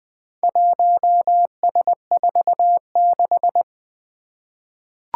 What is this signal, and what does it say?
Morse code, heard as a keyed tone, about 20 wpm, 700 Hz. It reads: 1S46
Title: Four Challenging Letter-Number Combinations 20wpm